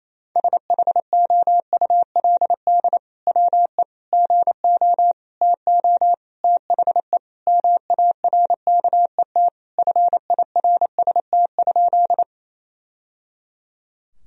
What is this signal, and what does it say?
Morse code, heard as a keyed tone, about 28 wpm, 700 Hz. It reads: SHOULD WE GO TO THE MARKET FIRST?
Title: Sentences from Top 1000 Words - Review Part 3 - 28wpm